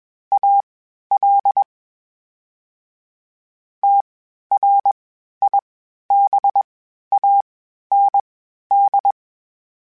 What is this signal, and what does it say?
Morse code, heard as a keyed tone, about 21 wpm, spaced as for 12 wpm, 800 Hz. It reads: AL TRIBAND
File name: (spaced) / FromSMs spaced mp3s / Antennas sp@25